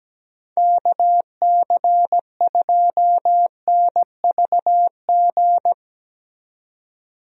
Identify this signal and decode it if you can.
Morse code, heard as a keyed tone, about 17 wpm, 700 Hz. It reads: KC2NVG
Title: USA Call Signs - 6 Characters - 17wpm